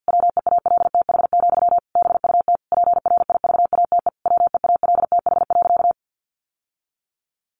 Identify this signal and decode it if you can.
Morse code, heard as a keyed tone, about 50 wpm, 700 Hz. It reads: WEALTH, BUT PRIVATE WEALTH.
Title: Sentences from Top 1200 Words 50wpm